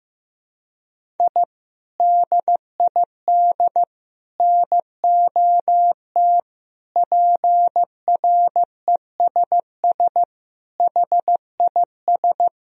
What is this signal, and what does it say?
Morse code, heard as a keyed tone, about 15 wpm, 700 Hz. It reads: I DID NOT PRESS HIS
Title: Sentences from Top 1000 Words - Review Part 1 - 15wpm